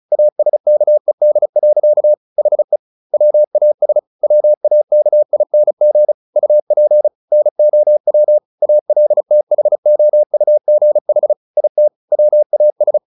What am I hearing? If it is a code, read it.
Morse code, heard as a keyed tone, about 34 wpm, 600 Hz. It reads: ASKED. HE WAS WAKING UP NOW ALTHOUGH IT WAS
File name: the old man and the sea / the old man and the sea 1 35wpm